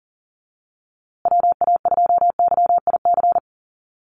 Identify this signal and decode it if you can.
Morse code, heard as a keyed tone, about 40 wpm, 700 Hz. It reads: WA2YIC
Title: USA Call Signs Repeated 3x 40wpm